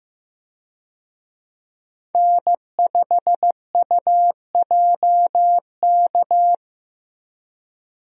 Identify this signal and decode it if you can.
Morse code, heard as a keyed tone, about 15 wpm, 700 Hz. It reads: N5UJK
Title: USA Call Signs - 5 Characters - Repeated 3x - 15wpm